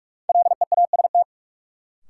Morse code, heard as a keyed tone, about 46 wpm, 700 Hz. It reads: LEAST
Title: Sentences from Top 700 Words - New Word Drill - 45wpm